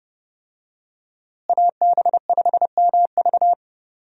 Morse code, heard as a keyed tone, about 32 wpm, 700 Hz. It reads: AB5MV
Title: USA Call Signs - 5 Characters - Repeated 3x - 30wpm